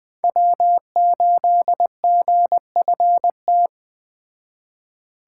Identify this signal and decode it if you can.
Morse code, heard as a keyed tone, about 20 wpm, 700 Hz. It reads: W8GFT